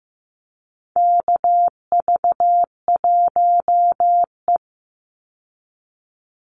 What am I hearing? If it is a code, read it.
Morse code, heard as a keyed tone, about 15 wpm, 700 Hz. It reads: KV1E
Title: Call Signs Repeated 2x - 15wpm